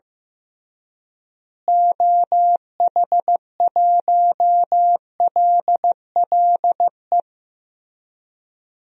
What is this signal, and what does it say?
Morse code, heard as a keyed tone, about 15 wpm, 700 Hz. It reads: OH1LLE
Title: DXCC-100 Call Signs - 6 Characters - Repeated 2x - 15wpm